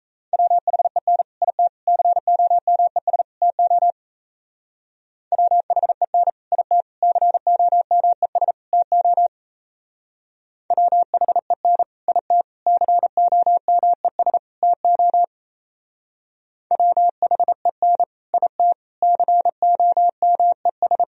Morse code, heard as a keyed tone, about 35 wpm, 700 Hz. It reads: WHEN IT COMES TO WHEN IT COMES TO WHEN IT COMES TO WHEN IT COMES
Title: Speed-Racing - Sets of 4 Words 28wpm